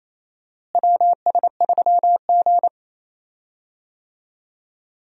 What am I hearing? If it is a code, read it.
Morse code, heard as a keyed tone, about 28 wpm, 700 Hz. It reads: WS3G
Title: USA Call Signs - 4 Characters - Repeated 3x - 28wpm